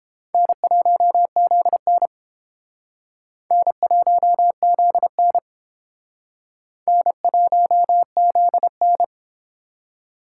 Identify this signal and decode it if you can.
Morse code, heard as a keyed tone, about 29 wpm, 700 Hz. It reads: N1ZN N1ZN N1ZN